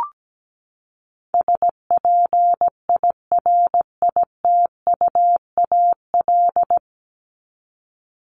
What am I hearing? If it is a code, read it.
Morse code, heard as a keyed tone, about 17 wpm, 700 Hz. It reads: SPIRITUAL
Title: Top 1501-1750 Words 17wpm